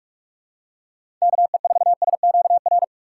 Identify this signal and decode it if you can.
Morse code, heard as a keyed tone, about 41 wpm, 700 Hz. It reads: KE4SQR
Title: USA Call Signs - 6 Characters - 3x Character Spacing - 45wpm